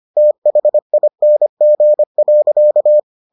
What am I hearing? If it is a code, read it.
Morse code, heard as a keyed tone, about 25 wpm, 600 Hz. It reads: THING.